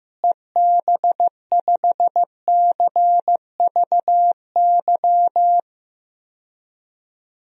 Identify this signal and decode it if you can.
Morse code, heard as a keyed tone, about 16 wpm, 700 Hz. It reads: EB5CVY
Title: DXCC-100 Call Signs - 6 Characters - Repeated 2x - 15wpm